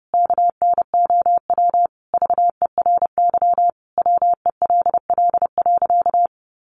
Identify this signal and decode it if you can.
Morse code, heard as a keyed tone, about 30 wpm, 700 Hz. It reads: KNOW VERY WELL.
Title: A Sound Of Thunder 30wpm